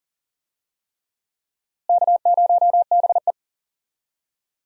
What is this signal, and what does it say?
Morse code, heard as a keyed tone, about 40 wpm, 700 Hz. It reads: K0BE
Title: USA Call Signs - 4 Characters - Repeated 3x - 40wpm